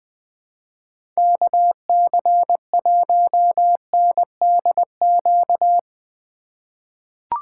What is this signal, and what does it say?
Morse code, heard as a keyed tone, about 20 wpm, 700 Hz. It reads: KC1NDQ
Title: 2023 POTA Activator Call Signs - 20wpm